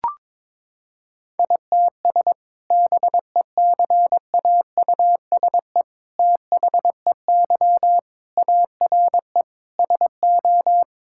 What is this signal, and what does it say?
Morse code, heard as a keyed tone, about 22 wpm, 700 Hz. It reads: ITS BECAUSE THEY ARE SO
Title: Sentences from Top 1000 Words - Review Part 6 - 22wpm